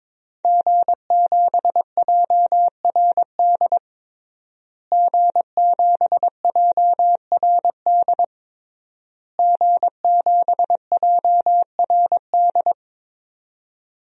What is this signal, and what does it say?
Morse code, heard as a keyed tone, about 21 wpm, 700 Hz. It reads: G7JRD G7JRD G7JRD
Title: DXCC-100 Call Signs - 5 Characters - Repeated 3x - 22wpm